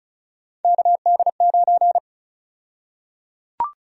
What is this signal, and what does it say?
Morse code, heard as a keyed tone, about 34 wpm, 700 Hz. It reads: KD9
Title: USA Call Sign Prefixes - Repeated 2x - 35wpm